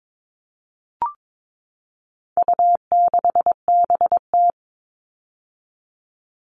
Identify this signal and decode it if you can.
Morse code, heard as a keyed tone, about 22 wpm, 700 Hz. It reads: U6BT
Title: All Cut Numbers - 4-Digit 22wpm